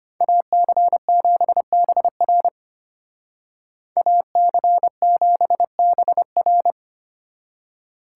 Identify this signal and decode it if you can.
Morse code, heard as a keyed tone, about 31 wpm, 700 Hz. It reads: AC7BR AC7BR
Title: Speed Racing - USA 2x2 Call Signs 25wpm